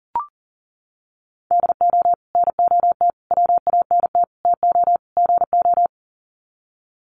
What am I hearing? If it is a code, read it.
Morse code, heard as a keyed tone, about 38 wpm, 700 Hz. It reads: DO NOT WANT TO GO
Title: Sets of 5 Words 40wpm